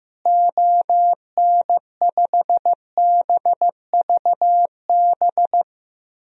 Morse code, heard as a keyed tone, about 14 wpm, 700 Hz. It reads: ON5BVB
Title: DXCC-100 Call Signs - 6 Characters - Repeated 3x - 15wpm